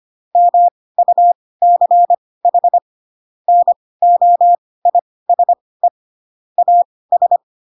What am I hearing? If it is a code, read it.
Morse code, heard as a keyed tone, about 25 wpm, spaced as for 18 wpm, 700 Hz. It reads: MUCH NOISE AS